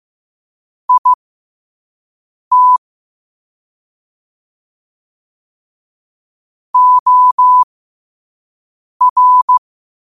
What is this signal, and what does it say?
Morse code, heard as a keyed tone, about 15 wpm, spaced as for 5 wpm, 1000 Hz. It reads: IT OR
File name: qotdfiles / 5wpm